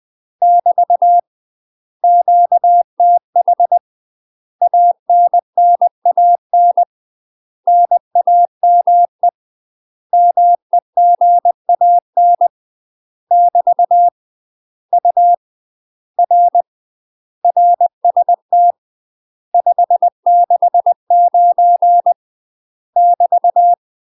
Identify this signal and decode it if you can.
Morse code, heard as a keyed tone, about 20 wpm, 700 Hz. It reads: <BT> QTH ANNAN NAME MEGAN <BT> U R RST 569 <BT>
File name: UK MASTER QSOs TRAINING / 20WPM / UK MASTER QSOs 0062